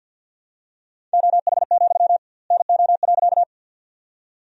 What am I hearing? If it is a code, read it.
Morse code, heard as a keyed tone, about 49 wpm, 700 Hz. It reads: OH, NO.
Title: 2x Word-Spacing - Sentences from Top 500 Words - 50wpm